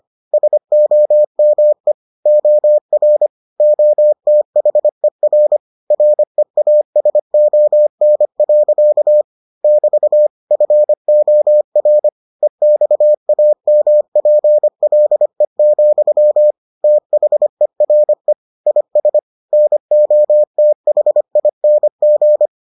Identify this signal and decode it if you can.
Morse code, heard as a keyed tone, about 25 wpm, 600 Hz. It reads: SOME OR OTHER REASON. <BT> FOR EXAMPLE, THERE IS NOTHING